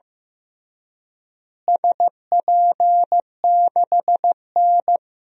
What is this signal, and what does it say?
Morse code, heard as a keyed tone, about 15 wpm, 700 Hz. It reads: SP6N
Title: DXCC-100 Call Signs - 4 Characters - Repeated 2x - 15wpm